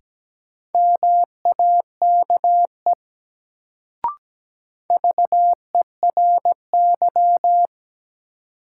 Morse code, heard as a keyed tone, about 17 wpm, 700 Hz. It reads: MAKE VERY
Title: Sets of 2 Words - New Word Drill - 17wpm